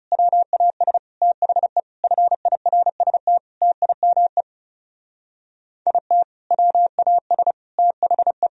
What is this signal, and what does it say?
Morse code, heard as a keyed tone, about 36 wpm, 700 Hz. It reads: WAS THE FIRST TIME IT WAS THE
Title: Speed-Racing - Sets of 5 Words 30wpm